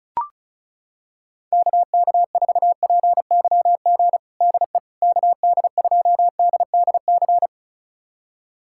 Morse code, heard as a keyed tone, about 36 wpm, 700 Hz. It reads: KK4PYG DE KD2DDC
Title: USA Callsign DE Callsign 35wpm